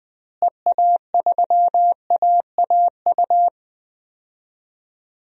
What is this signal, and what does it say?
Morse code, heard as a keyed tone, about 20 wpm, 700 Hz. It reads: EA3AAU
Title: DXCC-100 Call Signs - 6 Characters - Repeated 3x - 20wpm